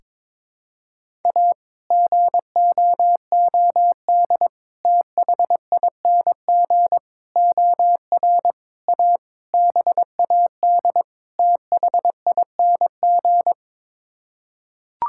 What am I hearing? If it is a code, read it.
Morse code, heard as a keyed tone, about 22 wpm, 700 Hz. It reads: A GOOD THING OR A BAD THING
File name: practice / 22wpm / Sets of 7 Words - 22wpm